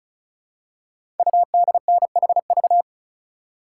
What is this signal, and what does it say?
Morse code, heard as a keyed tone, about 36 wpm, 690 Hz. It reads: UDNHV
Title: Code Groups - 5 Letters - 35wpm